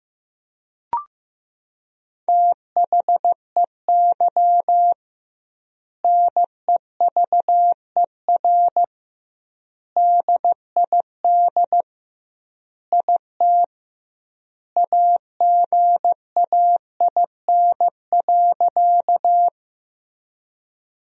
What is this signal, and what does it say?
Morse code, heard as a keyed tone, about 16 wpm, 700 Hz. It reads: THEY NEVER DID IT AGAIN.
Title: 2x Word-Spacing - Sentences from Top 300 Words - 15wpm